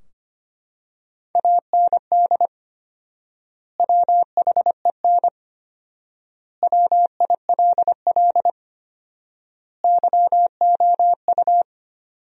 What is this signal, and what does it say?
Morse code, heard as a keyed tone, about 25 wpm, 700 Hz. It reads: AND WHEN WILL YOU
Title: 4x Word-Spacing - Sentences from Top 100 Words - 25wpm